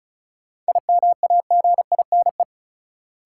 Morse code, heard as a keyed tone, about 36 wpm, 700 Hz. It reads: IMAGINE
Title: Sentences from Top 1400 Words - New Word Drill - 35wpm